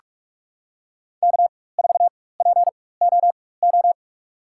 Morse code, heard as a keyed tone, about 46 wpm, spaced as for 21 wpm, 700 Hz. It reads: K4POO
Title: USA Call Signs - 5 Characters - 4x Character Spacing - 45wpm